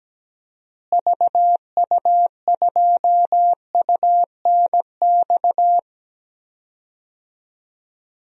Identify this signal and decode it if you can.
Morse code, heard as a keyed tone, about 17 wpm, 700 Hz. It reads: VU2UNX
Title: DXCC-100 Call Signs - 6 Characters - Repeated 3x - 17wpm